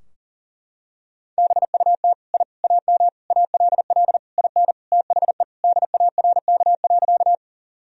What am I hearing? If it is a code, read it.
Morse code, heard as a keyed tone, about 40 wpm, 700 Hz. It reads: BUT I AM ALL IN THE DARK.
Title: Sentences from Top 1000 Words - Review Part 2 - 40wpm